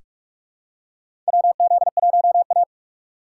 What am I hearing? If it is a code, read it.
Morse code, heard as a keyed tone, about 44 wpm, 700 Hz. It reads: WZ1A
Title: USA Call Signs - 4 Characters - 6x Character Spacing - 45wpm